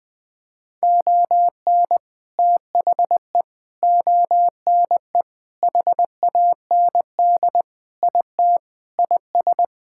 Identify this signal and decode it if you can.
Morse code, heard as a keyed tone, about 20 wpm, 700 Hz. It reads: ON THE ONE HAND IT IS
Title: Sentences with 8-Word Phrases - Spoken Beginning - 20wpm